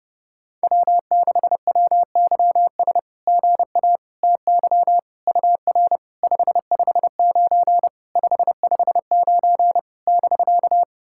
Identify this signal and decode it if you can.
Morse code, heard as a keyed tone, about 28 wpm, 690 Hz. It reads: W6WYS GA TY UR 559 559 <BK>
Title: SOTA QSO Practice 30wpm